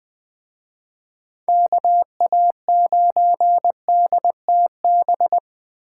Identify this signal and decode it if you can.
Morse code, heard as a keyed tone, about 20 wpm, 700 Hz. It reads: KA9DTB
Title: Speed Racing - USA 2x3 Call Signs 17wpm